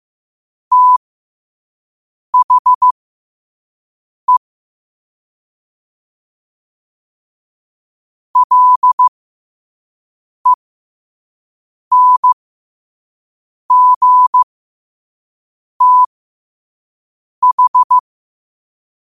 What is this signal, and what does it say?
Morse code, heard as a keyed tone, about 15 wpm, spaced as for 5 wpm, 1000 Hz. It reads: THE LENGTH
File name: qotdfiles / 5wpm